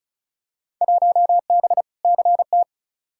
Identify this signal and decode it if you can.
Morse code, heard as a keyed tone, about 36 wpm, 700 Hz. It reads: 1B CT
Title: Field Day Exchanges Encore - 35wpm